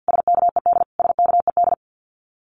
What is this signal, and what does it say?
Morse code, heard as a keyed tone, about 50 wpm, 700 Hz. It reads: SKED SKED
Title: Abbreviations Repeated 3x - 50wpm